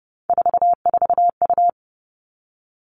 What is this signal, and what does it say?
Morse code, heard as a keyed tone, about 30 wpm, 700 Hz. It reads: 44U
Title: All Cut Numbers - 3-Digit 30wpm